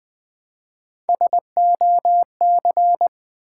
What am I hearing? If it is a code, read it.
Morse code, heard as a keyed tone, about 20 wpm, 700 Hz. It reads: SOC
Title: Build A Word - 1 Letter At A Time - 4-Letter Words - 20wpm